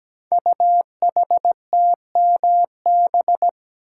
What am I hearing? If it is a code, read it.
Morse code, heard as a keyed tone, about 17 wpm, 700 Hz. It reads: UHTMB
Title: Code Groups - 5 Letter:Numbers - 17wpm